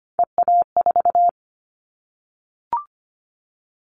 Morse code, heard as a keyed tone, about 25 wpm, 700 Hz. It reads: EA4
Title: Callsign Prefixes for Top 25 DXCC - 25wpm